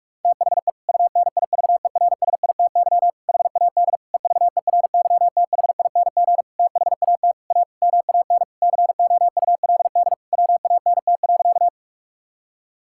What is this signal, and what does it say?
Morse code, heard as a keyed tone, about 46 wpm, 700 Hz. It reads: THE UNIVERSITY HAD EVERYTHING THAT A MAN COULD WANT.
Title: Sentences from Top 1000 Words - Review Part 5 - 45wpm